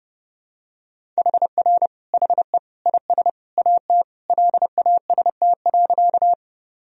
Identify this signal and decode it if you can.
Morse code, heard as a keyed tone, about 30 wpm, 700 Hz. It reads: HR HE IS AT LAST.
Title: Sentences from Top 300 Words and Abbreviations 30wpm